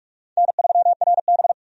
Morse code, heard as a keyed tone, about 46 wpm, 700 Hz. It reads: N3RB
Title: USA Call Signs - 4 Characters - 45wpm